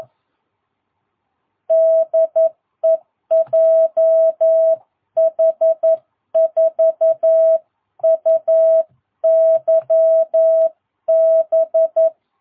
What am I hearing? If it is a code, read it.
Morse code, heard as a keyed tone, about 11 wpm, 660 Hz. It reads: DE JH4UYB
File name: jh8cbh / 2022cw